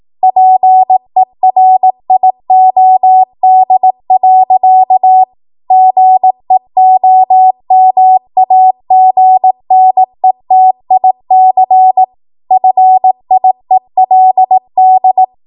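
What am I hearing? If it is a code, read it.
Morse code, heard as a keyed tone, about 18 wpm, 750 Hz. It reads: PERIOD. GEOMAGNETIC FIELD
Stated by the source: ARRL 18 wpm Morse code practice archive transmission.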